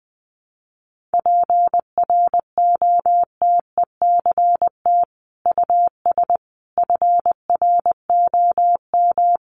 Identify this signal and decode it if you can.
Morse code, heard as a keyed tone, about 20 wpm, 700 Hz. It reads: PROTECT US FROM
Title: Sentences from Top 900 Words 20wpm